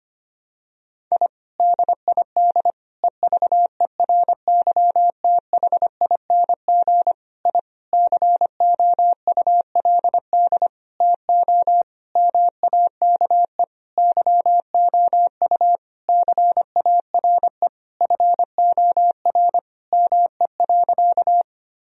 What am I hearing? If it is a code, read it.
Morse code, heard as a keyed tone, about 25 wpm, 700 Hz. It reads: I DID EVERYTHING I COULD TO MAKE YOU CARE FOR ME.
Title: Sentences from Top 1000 Words - Review Part 1 - 25wpm